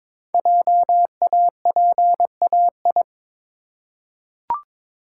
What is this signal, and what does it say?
Morse code, heard as a keyed tone, about 23 wpm, 700 Hz. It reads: JAPAI
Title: Code Groups - 5 Letter:Numbers - 22wpm